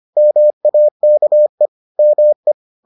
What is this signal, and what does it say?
Morse code, heard as a keyed tone, about 25 wpm, 600 Hz. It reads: MAKE ME